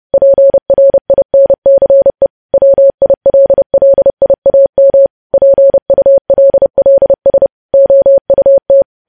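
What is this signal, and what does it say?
Morse code, heard as a keyed tone, about 30 wpm, 550 Hz. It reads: PRINCE WILLIAM PULLS OUT
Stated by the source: Morse code news headlines sent at 30, preceded by vvv vvv.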